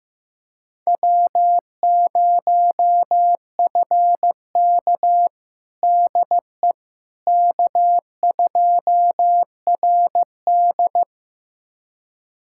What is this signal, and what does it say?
Morse code, heard as a keyed tone, about 16 wpm, 700 Hz. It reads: W0FK DE K2RD
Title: USA Callsign DE Callsign - 4 Characters - 15wpm